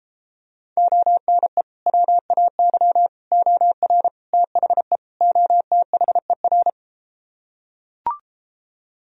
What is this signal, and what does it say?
Morse code, heard as a keyed tone, about 33 wpm, 700 Hz. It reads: ONE WAY OR THE OTHER